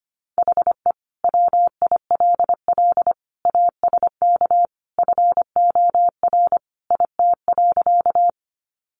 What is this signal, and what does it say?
Morse code, heard as a keyed tone, about 25 wpm, 700 Hz. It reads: HE WILL ASK FOR IT.
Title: Sentences from Top 200 Words 25wpm